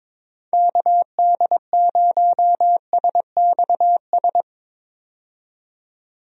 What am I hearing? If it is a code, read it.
Morse code, heard as a keyed tone, about 22 wpm, 700 Hz. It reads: KD0SXS
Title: USA Call Signs - 6 Characters - 22wpm